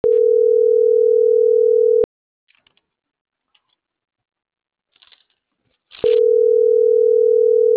Ringback tone (North America), 440 + 480 Hz.